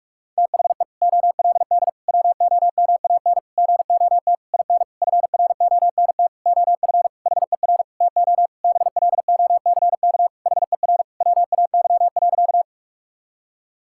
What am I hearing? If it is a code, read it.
Morse code, heard as a keyed tone, about 46 wpm, 700 Hz. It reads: THE OLD WOMAN GOT IN FRONT OF HER TO BLOCK HER WAY.
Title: Sentences from Top 1600 Words 45wpm